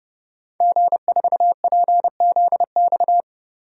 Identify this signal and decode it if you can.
Morse code, heard as a keyed tone, about 31 wpm, 700 Hz. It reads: G4PZX
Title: DXCC-100 Call Signs - 5 Characters - Repeated 3x - 30wpm